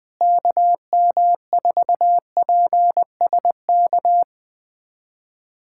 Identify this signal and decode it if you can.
Morse code, heard as a keyed tone, about 20 wpm, 700 Hz. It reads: KM4PSK
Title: USA Callsign DE Callsign 20wpm